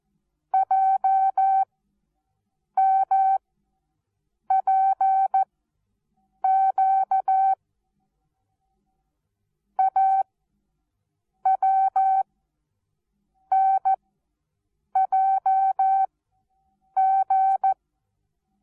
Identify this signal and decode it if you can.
Morse code, heard as a keyed tone, about 14 wpm, 770 Hz. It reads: J M P Q A W N J G
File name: Onlinelehrgang / morsen / DJ4UF Morsen19